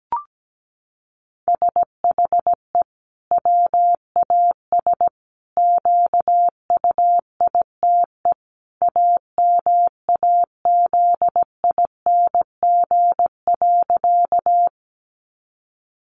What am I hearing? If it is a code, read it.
Morse code, heard as a keyed tone, about 17 wpm, 700 Hz. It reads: SHE WAS QUITE AMAZING.
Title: Sentences from Top 1500 Words 17wpm